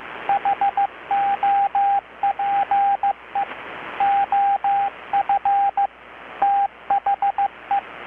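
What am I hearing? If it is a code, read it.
Morse code, heard as a keyed tone, about 15 wpm, 780 Hz, with radio noise. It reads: HOPE OF THE
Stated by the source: Morse from KPH's Night of Nights XX maritime radiotelegraphy event, received in Chiapas, Mexico about 3,800 km away on a Kenwood R-600 with a grounded 10 m long-wire antenna.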